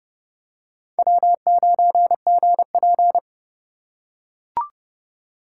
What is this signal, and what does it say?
Morse code, heard as a keyed tone, about 28 wpm, 700 Hz. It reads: W9GP